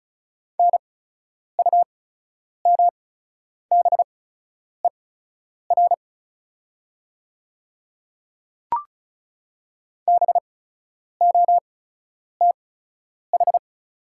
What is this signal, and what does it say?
Morse code, heard as a keyed tone, about 34 wpm, spaced as for 10 wpm, 700 Hz. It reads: NUMBER BOTH
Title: 8x Character Spacing - Top 300 Words - 35wpm